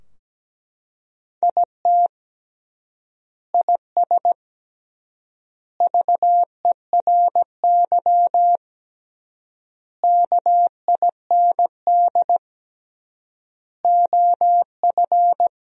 Morse code, heard as a keyed tone, about 17 wpm, 700 Hz. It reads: IT IS VERY KIND OF
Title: 3x Word-Spacing - Sentences from Top 200 Words - 17wpm